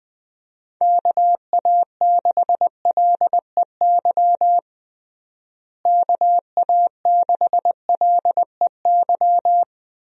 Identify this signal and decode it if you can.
Morse code, heard as a keyed tone, about 20 wpm, 700 Hz. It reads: KA6LEY KA6LEY
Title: USA Call Signs - 6 Characters - Repeated 2x - 20wpm